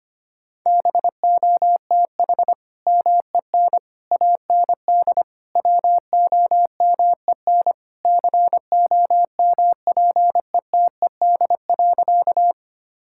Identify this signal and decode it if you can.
Morse code, heard as a keyed tone, about 25 wpm, 700 Hz. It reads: BOTH MEN AND WOMEN COMPETED.
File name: practice / 25wpm / Sentences with 3-Word Phrases - 25wpm